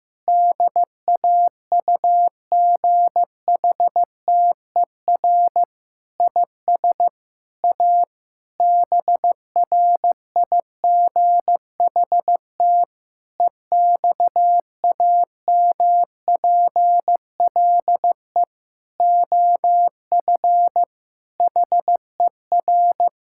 Morse code, heard as a keyed tone, about 16 wpm, 700 Hz. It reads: DAUGHTER IS A BRIGHT EXAMPLE OF HER
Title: Sentences from Top 1600 Words 15wpm